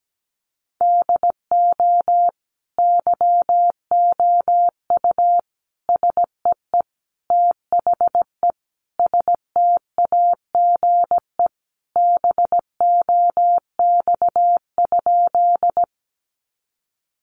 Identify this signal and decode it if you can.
Morse code, heard as a keyed tone, about 17 wpm, 700 Hz. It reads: DO YOU SEE THE STAGE BOX?
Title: Sentences from Top 1200 Words 17wpm